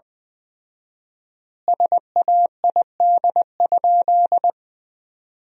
Morse code, heard as a keyed tone, about 21 wpm, 700 Hz. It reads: SAID?
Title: 4x Word-Spacing - Sentences from Top 100 Words - 20wpm